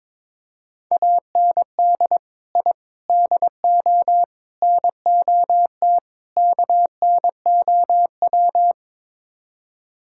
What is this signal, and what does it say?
Morse code, heard as a keyed tone, about 21 wpm, 700 Hz. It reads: AND I DO NOT KNOW
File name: practice / 22wpm / Build A Phrase - 5-Word Phrases - 22wpm